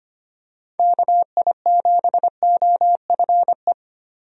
Morse code, heard as a keyed tone, about 25 wpm, 700 Hz. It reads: KI7OFE
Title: USA Call Signs - 6 Characters - Repeated 3x - 25wpm